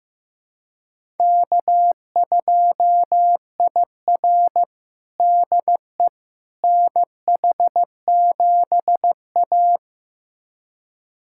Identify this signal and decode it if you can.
Morse code, heard as a keyed tone, about 16 wpm, 700 Hz. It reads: K2IR DE NH7A
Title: USA Callsign DE Callsign - 4 Characters - 15wpm